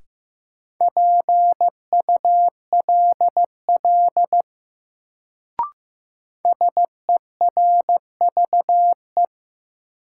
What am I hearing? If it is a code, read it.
Morse code, heard as a keyed tone, about 15 wpm, 700 Hz. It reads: PULL SERVE
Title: Sentences from Top 400 Words - New Word Drill - 15wpm